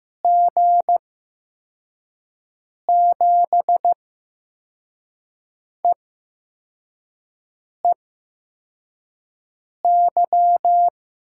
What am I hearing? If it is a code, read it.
Morse code, heard as a keyed tone, about 15 wpm, spaced as for 4 wpm, 700 Hz. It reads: G7EEY
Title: USA Call Signs - 8x Character Spacing - 15wpm